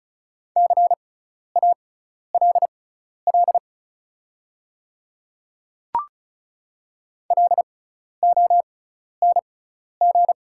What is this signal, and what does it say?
Morse code, heard as a keyed tone, about 34 wpm, spaced as for 12 wpm, 700 Hz. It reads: CALL LONG
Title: 6x Character Spacing - Top 100 Words - 35wpm